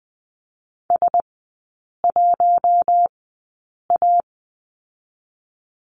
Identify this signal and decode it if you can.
Morse code, heard as a keyed tone, about 20 wpm, spaced as for 8 wpm, 700 Hz. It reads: S1A
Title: 041 Lesson 8e - Number 1 - Three Characters 20wpm